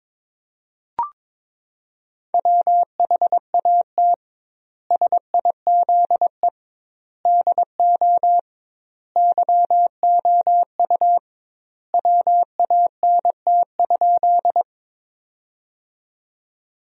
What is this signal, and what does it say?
Morse code, heard as a keyed tone, about 22 wpm, 700 Hz. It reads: WHAT SIZE DO YOU WANT?
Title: 2x Word-Spacing - Sentences from Top 400 Words - 22wpm